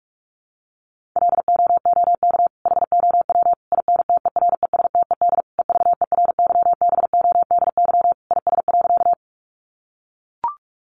Morse code, heard as a keyed tone, about 44 wpm, 700 Hz. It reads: LOOK HOW INTERESTED EVERYBODY IS.
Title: Sentences from Top 1400 Words 45wpm